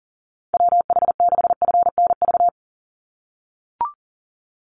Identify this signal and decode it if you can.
Morse code, heard as a keyed tone, about 37 wpm, 700 Hz. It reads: WH6FNV
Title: USA Call Signs 40wpm